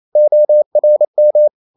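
Morse code, heard as a keyed tone, about 28 wpm, 600 Hz. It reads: ORM
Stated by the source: Danish short words sent in Morse at 28 wpm, from CW med Gnister.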